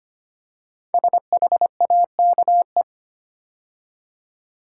Morse code, heard as a keyed tone, about 25 wpm, 700 Hz. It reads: SHAKE
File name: 5 Letter Words - First 3 Letters Given - 25wpm